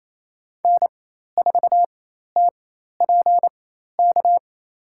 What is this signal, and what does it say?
Morse code, heard as a keyed tone, about 27 wpm, spaced as for 13 wpm, 700 Hz. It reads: N4TPK
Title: USA Call Signs - 4x Character Spacing - 28wpm